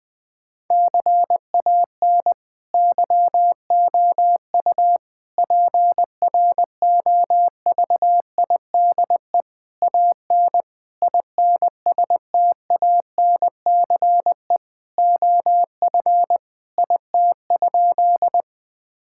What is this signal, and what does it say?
Morse code, heard as a keyed tone, about 20 wpm, 700 Hz. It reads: CAN YOU PROVIDE AN INSTANCE OF IT?
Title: Sentences from Top 1000 Words - Review Part 5 - 20wpm